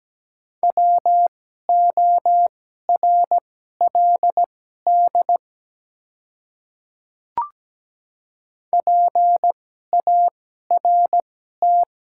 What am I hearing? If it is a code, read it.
Morse code, heard as a keyed tone, about 17 wpm, spaced as for 12 wpm, 700 Hz. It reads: WORLD PART
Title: 2x Character Spacing - Top 200 Words - 17wpm